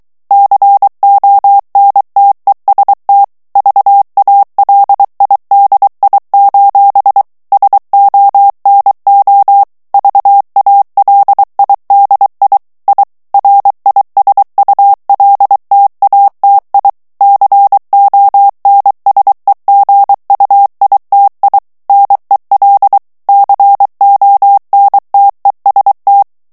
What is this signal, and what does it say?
Morse code, heard as a keyed tone, about 23 wpm, 800 Hz. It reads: CONTEST VALIDI: SONO VALIDI I RISULTATI CONSEGUITI NEL CONTEST
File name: ITALIEN texte clair 2 vitesse 22 mots minute